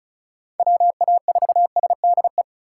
Morse code, heard as a keyed tone, about 34 wpm, 700 Hz. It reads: WA4SDE
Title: USA Call Signs - 6 Characters - 3x Character Spacing - 35wpm